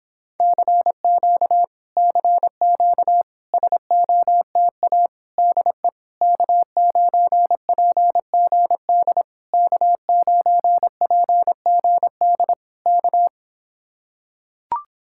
Morse code, heard as a keyed tone, about 26 wpm, 700 Hz. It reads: CQ CQ SOTA DE K9PGD K9PGD K